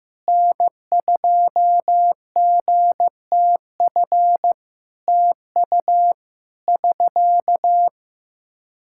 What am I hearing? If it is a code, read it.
Morse code, heard as a keyed tone, about 15 wpm, 700 Hz. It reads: N2GTF TU <SK>
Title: Practice QSO - Exchange 4 - Ending - 15wpm